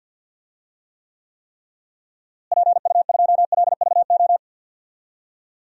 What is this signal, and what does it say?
Morse code, heard as a keyed tone, about 53 wpm, 700 Hz. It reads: PU2LVO
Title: DXCC-100 Call Signs - 6 Characters - Repeated 3x - 50wpm